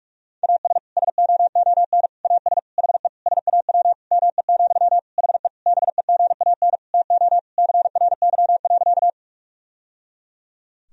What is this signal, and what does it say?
Morse code, heard as a keyed tone, about 46 wpm, 700 Hz. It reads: AS SOON AS HE SAW ME, HE BEGAN TO CRY.
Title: Sentences from Top 1000 Words - Review Part 2 - 45wpm